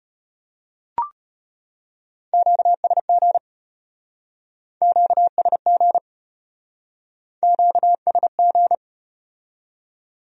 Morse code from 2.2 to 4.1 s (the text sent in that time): QSG